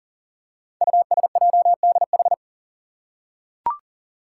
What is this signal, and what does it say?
Morse code, heard as a keyed tone, about 39 wpm, 700 Hz. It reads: USJDH
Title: Code Groups - 5 Letters - 40wpm